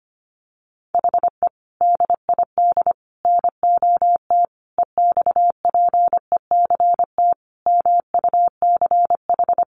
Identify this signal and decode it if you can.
Morse code, heard as a keyed tone, about 25 wpm, 700 Hz. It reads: HE DID NOT EXPECT MUCH
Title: Sentences from Top 1100 Words 25wpm